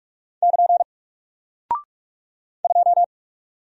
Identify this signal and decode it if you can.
Morse code, heard as a keyed tone, about 48 wpm, 700 Hz. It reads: <KN> 2
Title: Single Character (Letter:Number:Punctuation:Prosign) Alt 45wpm